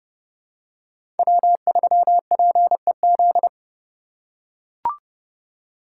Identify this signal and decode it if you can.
Morse code, heard as a keyed tone, about 32 wpm, 700 Hz. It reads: W3PEZ
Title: USA Call Signs - 3x Character Spacing - 30wpm